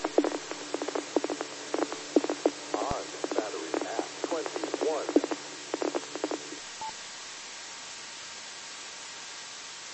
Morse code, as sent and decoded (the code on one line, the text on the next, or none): none